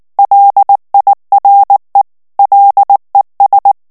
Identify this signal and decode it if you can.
Morse code, heard as a keyed tone, about 19 wpm, 800 Hz. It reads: LIRE LES
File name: CW / FICHIERS-MP3 / FRANCAIS TEXTE CLAIR 2 VITESSE 18 MOTS MINUTES